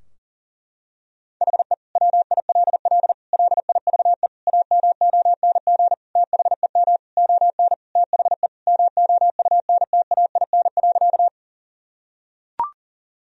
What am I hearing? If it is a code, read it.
Morse code, heard as a keyed tone, about 37 wpm, 700 Hz. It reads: HE WILL LIVE AMONG THEM ON THE MOUNTAIN.
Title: Sentences from Top 1000 Words - Review Part 3 - 40wpm